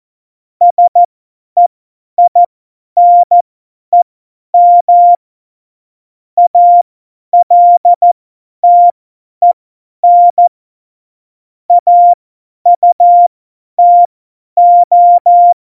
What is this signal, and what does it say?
Morse code, heard as a keyed tone, about 14 wpm, spaced as for 10 wpm, 700 Hz. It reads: SEINEM ALTEN AUTO
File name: Morsekurs / Witze / Witz 14 Juli 2020, 50 F70